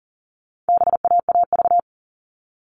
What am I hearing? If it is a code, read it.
Morse code, heard as a keyed tone, about 40 wpm, 700 Hz. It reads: BAAV
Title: All Cut Numbers - 4-Digit 40wpm